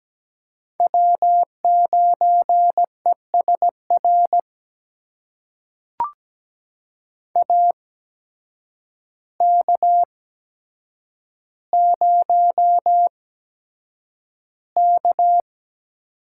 Morse code, heard as a keyed tone, about 17 wpm, spaced as for 8 wpm, 700 Hz. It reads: W9ESR AK0K
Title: USA Call Signs - 8x Character Spacing - 17wpm